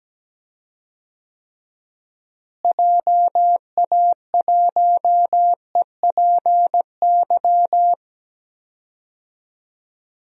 Morse code, heard as a keyed tone, about 17 wpm, 700 Hz. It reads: JA1EPY
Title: DXCC-100 Call Signs - 6 Characters - Repeated 2x - 17wpm